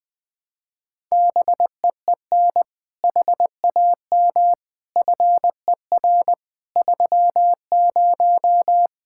Morse code, heard as a keyed tone, about 21 wpm, 700 Hz. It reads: BEEN HAM FER 30
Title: Speed-Racing Common QSO Phrases - 20wpm